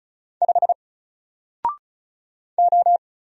Morse code, heard as a keyed tone, about 36 wpm, 700 Hz. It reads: 5 O
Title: Single Letter-Number Alt 35wpm